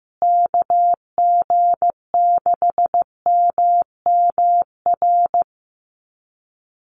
Morse code, heard as a keyed tone, about 15 wpm, 700 Hz. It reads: KG6MMR
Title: USA Call Signs Repeated 3x 15wpm